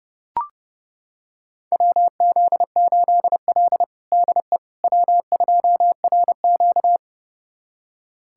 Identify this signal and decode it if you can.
Morse code, heard as a keyed tone, about 30 wpm, 700 Hz. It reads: WZ8L DE W2RQ
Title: USA Callsign DE Callsign - 4 Characters - 30wpm